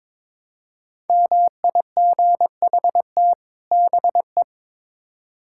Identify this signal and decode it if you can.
Morse code, heard as a keyed tone, about 22 wpm, 700 Hz. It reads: MIGHT BE
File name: Build A Phrase - 2-Word Phrases - 22wpm